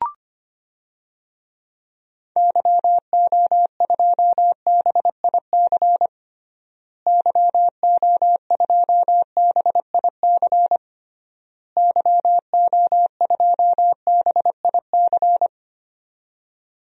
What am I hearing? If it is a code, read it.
Morse code, heard as a keyed tone, about 25 wpm, 700 Hz. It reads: YO2BIC YO2BIC YO2BIC
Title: DXCC-100 Call Signs - 6 Characters - Repeated 3x - 25wpm